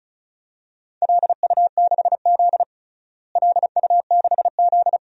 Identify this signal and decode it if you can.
Morse code, heard as a keyed tone, about 36 wpm, 700 Hz. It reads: LU6Z LU6Z
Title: DXCC-100 Call Signs - 4 Characters - Repeated 2x - 35wpm